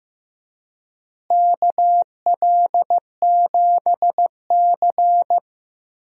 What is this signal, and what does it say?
Morse code, heard as a keyed tone, about 15 wpm, 700 Hz. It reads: KL7C
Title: DXCC-100 Call Signs - 4 Characters - Repeated 3x - 15wpm